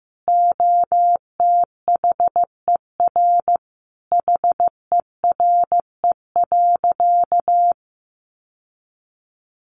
Morse code, heard as a keyed tone, about 14 wpm, 700 Hz. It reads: OTHER HERE.
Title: Sentences from Top 1200 Words 15wpm